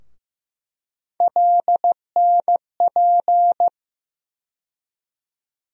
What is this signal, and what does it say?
Morse code, heard as a keyed tone, about 15 wpm, 700 Hz. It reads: LNP
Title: USA Call Sign Postfixes - 3 Letters 15wpm